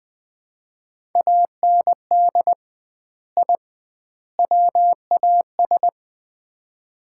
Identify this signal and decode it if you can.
Morse code, heard as a keyed tone, about 20 wpm, 700 Hz. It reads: AND I WAS
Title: 2x Word-Spacing - Sets of 3 Words - 20wpm